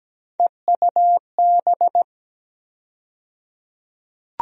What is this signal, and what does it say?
Morse code, heard as a keyed tone, about 17 wpm, 700 Hz. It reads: EUB
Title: USA Call Sign Postfixes - 3 Letters 17wpm